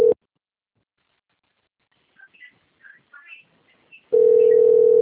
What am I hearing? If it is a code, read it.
Ringback tone (North America), 440 + 480 Hz.
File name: out- - - - - .22044